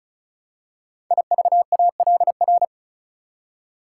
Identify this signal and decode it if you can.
Morse code, heard as a keyed tone, about 36 wpm, 700 Hz. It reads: IVALR
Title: Code Groups - 5 Letters - 35wpm